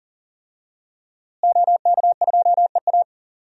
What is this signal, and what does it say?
Morse code, heard as a keyed tone, about 40 wpm, 700 Hz. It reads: OK2EA
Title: DXCC-100 Call Signs - 5 Characters - Repeated 2x - 40wpm